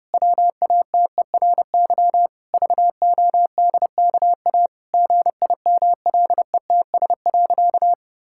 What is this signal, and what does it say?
Morse code, heard as a keyed tone, about 30 wpm, 700 Hz. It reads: WATERY VODKA GIMLETS.
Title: Pangram Challenge v2 - 30wpm